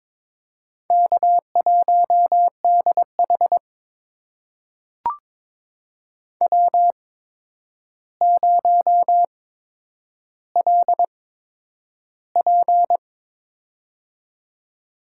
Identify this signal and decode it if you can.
Morse code, heard as a keyed tone, about 22 wpm, spaced as for 9 wpm, 700 Hz. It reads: K1DH W0LP
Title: USA Call Signs - 4 Characters - 8x Character Spacing - 22wpm